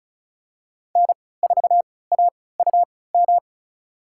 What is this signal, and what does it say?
Morse code, heard as a keyed tone, about 36 wpm, spaced as for 20 wpm, 690 Hz. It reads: N4AUM
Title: USA Call Signs - 5 Characters - 3x Character Spacing - 35wpm